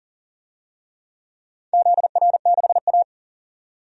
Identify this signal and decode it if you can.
Morse code, heard as a keyed tone, about 37 wpm, 700 Hz. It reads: ZR6A
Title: DXCC-100 Call Signs - 4 Characters - 40wpm